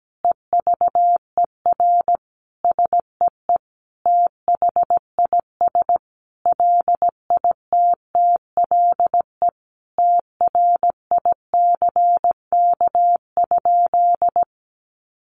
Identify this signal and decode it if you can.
Morse code, heard as a keyed tone, about 17 wpm, 700 Hz. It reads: EVER SEE THIS LITTLE TRICK?
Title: Sentences from Top 1500 Words 17wpm